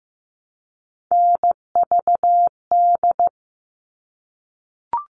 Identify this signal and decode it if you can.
Morse code, heard as a keyed tone, about 15 wpm, 700 Hz. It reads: NVD
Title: All Cut Numbers - 3-Digit 15wpm